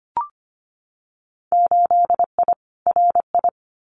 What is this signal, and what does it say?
Morse code, heard as a keyed tone, about 25 wpm, 700 Hz. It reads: 8I RI
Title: Winter Field Day Exchanges Encore 25wpm